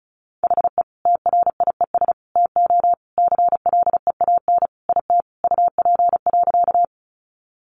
Morse code, heard as a keyed tone, about 34 wpm, 700 Hz. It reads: HE TRIES TO CLEAN IT UP.
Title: Additional Easy Phrases 35wpm